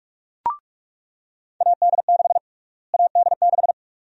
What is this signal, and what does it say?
Morse code, heard as a keyed tone, about 46 wpm, 700 Hz. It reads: AD6 AD6
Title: USA Call Sign Prefixes - Repeated 2x - 45wpm